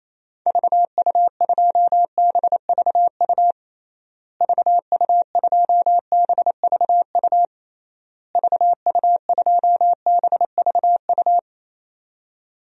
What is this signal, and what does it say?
Morse code, heard as a keyed tone, about 28 wpm, 700 Hz. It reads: VU2BVU VU2BVU VU2BVU
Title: DXCC-100 Call Signs - 6 Characters - Repeated 3x - 28wpm